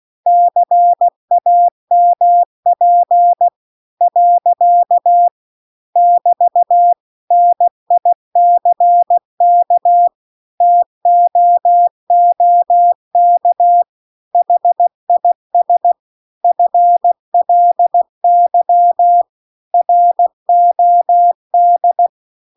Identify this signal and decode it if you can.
Morse code, heard as a keyed tone, about 16 wpm, 700 Hz. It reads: CAMP . <BT> NICK TOOK HIS FLY ROD